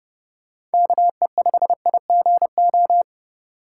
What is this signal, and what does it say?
Morse code, heard as a keyed tone, about 30 wpm, 700 Hz. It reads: KE5IGO
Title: USA Call Signs - 6 Characters - 3x Character Spacing - 30wpm